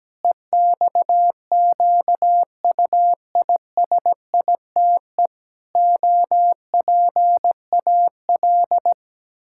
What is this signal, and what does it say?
Morse code, heard as a keyed tone, about 17 wpm, 700 Hz. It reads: EXQUISITE OPAL
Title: Pangram Challenge v2 - 17wpm